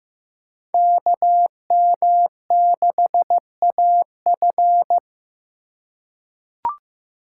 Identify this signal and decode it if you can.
Morse code, heard as a keyed tone, about 15 wpm, 700 Hz. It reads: KM6AF
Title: USA Call Signs - 2x Character Spacing - 15wpm